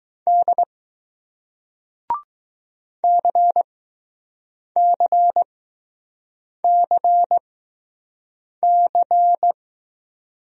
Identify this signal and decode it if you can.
Morse code, heard as a keyed tone, about 20 wpm, spaced as for 6 wpm, 700 Hz. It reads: D CCCC